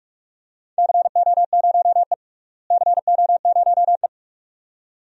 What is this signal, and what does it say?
Morse code, heard as a keyed tone, about 41 wpm, 700 Hz. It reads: CO0E CO0E
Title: DXCC-100 Call Signs - 4 Characters - Repeated 2x - 45wpm